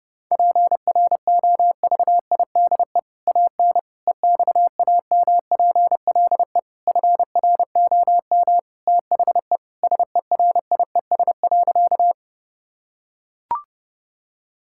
Morse code, heard as a keyed tone, about 32 wpm, 700 Hz. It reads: PROVIDE AN EXAMPLE FROM THE SERIES.
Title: Sentences from Top 1000 Words - Review Part 5 - 30wpm